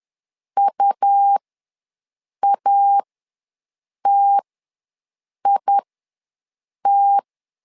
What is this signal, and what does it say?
Morse code, heard as a keyed tone, about 10 wpm, spaced as for 6 wpm, 800 Hz. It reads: UATIT
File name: LEZIONI / 2 UHDN / 2 UH ws 10wpm 2029